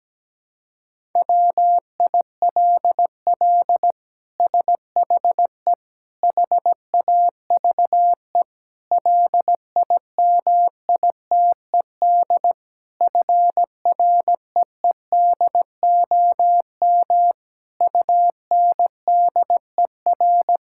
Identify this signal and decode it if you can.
Morse code, heard as a keyed tone, about 17 wpm, 700 Hz. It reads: WILL SHE HAVE LIMITED FREEDOM UNDER
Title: Sentences from Top 1000 Words - Review Part 6 - 17wpm